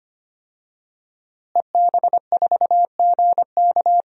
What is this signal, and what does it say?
Morse code, heard as a keyed tone, about 25 wpm, 700 Hz. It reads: EB4GK
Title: DXCC-100 Call Signs - 5 Characters - Repeated 2x - 25wpm